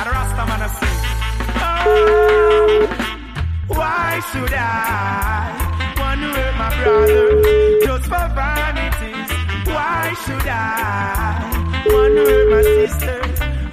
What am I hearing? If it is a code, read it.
Ringback tone (Europe, 425 Hz), with noise.